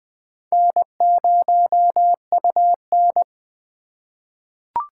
Morse code, heard as a keyed tone, about 20 wpm, 700 Hz. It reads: N0UN
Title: USA Call Signs - 4 Characters - 4x Character Spacing - 20wpm